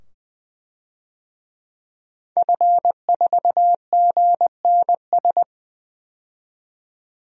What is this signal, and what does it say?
Morse code, heard as a keyed tone, about 20 wpm, 700 Hz. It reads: F4GNS
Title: DXCC-100 Call Signs - 5 Characters - 20wpm